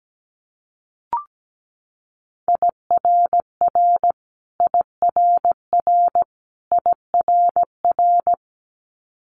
Morse code, heard as a keyed tone, about 17 wpm, 700 Hz. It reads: IRR IRR IRR
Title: Multi-letter Phonemes Repeated 3x 17wpm